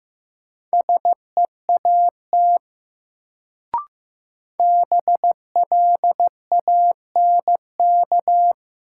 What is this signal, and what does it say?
Morse code, heard as a keyed tone, about 15 wpm, 700 Hz. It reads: SEAT BLANK
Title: Sentences from Top 1500 Words - New Word Drill - 15wpm